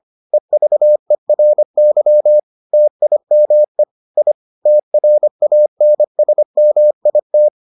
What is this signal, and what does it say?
Morse code, heard as a keyed tone, about 25 wpm, 600 Hz. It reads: EVERY TIME I TRANSMIT